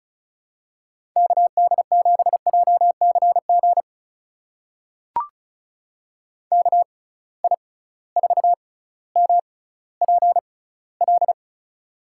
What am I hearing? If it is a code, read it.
Morse code, heard as a keyed tone, about 36 wpm, spaced as for 18 wpm, 700 Hz. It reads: KD7JCG KI4MPL
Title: USA Call Signs - 6x Character Spacing - 35wpm